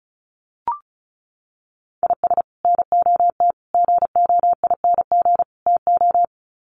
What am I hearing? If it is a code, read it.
Morse code, heard as a keyed tone, about 34 wpm, 700 Hz. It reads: IS NOT GOING TO
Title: Sets of 4 Words 35wpm